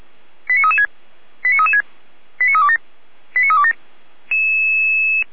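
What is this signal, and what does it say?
Five-tone selective call (ZVEI), heard as quick runs of five tones, tones of about 70 ms. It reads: 89336 89336 89327 89327
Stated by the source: ZVEI five-tone call 89325 received with an SDR.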